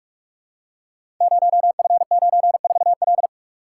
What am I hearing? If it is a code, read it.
Morse code, heard as a keyed tone, about 44 wpm, 700 Hz. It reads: 0F94L
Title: Code Groups - 5 Letter:Numbers - 45wpm